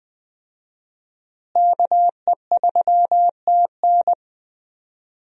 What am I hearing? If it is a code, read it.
Morse code, heard as a keyed tone, about 20 wpm, 700 Hz. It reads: KE3TN
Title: USA Call Signs - 5 Characters - Repeated 3x - 20wpm